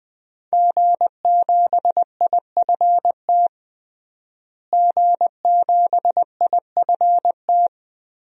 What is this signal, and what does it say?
Morse code, heard as a keyed tone, about 21 wpm, 700 Hz. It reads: G7IFT G7IFT
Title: DXCC-100 Call Signs - 5 Characters - Repeated 3x - 20wpm